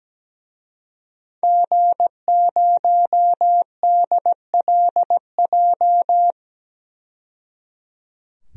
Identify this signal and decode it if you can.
Morse code, heard as a keyed tone, about 17 wpm, 700 Hz. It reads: G0DLJ
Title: DXCC-100 Call Signs - 5 Characters - 17wpm